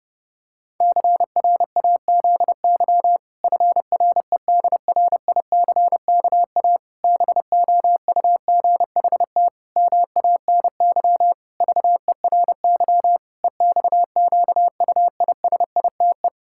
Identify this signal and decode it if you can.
Morse code, heard as a keyed tone, about 30 wpm, 700 Hz. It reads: CRAZY FREDRICKA BOUGHT MANY VERY EXQUISITE
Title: Pangram Challenge v2 - 30wpm